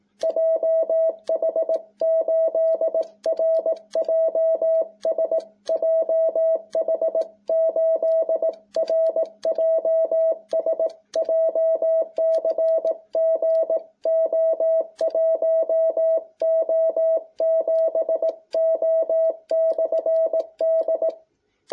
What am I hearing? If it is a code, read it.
Morse code, heard as a keyed tone, about 18 wpm, 660 Hz. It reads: JH8RJS JH8RJS JCG O1O7O/D